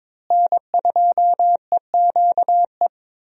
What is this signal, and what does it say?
Morse code, heard as a keyed tone, about 22 wpm, 700 Hz. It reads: N2EQE
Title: USA Call Signs - 5 Characters - Repeated 2x - 22wpm